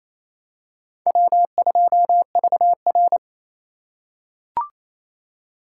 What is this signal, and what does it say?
Morse code, heard as a keyed tone, about 28 wpm, 700 Hz. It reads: W2VR
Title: USA Call Signs - 4 Characters - 6x Character Spacing - 28wpm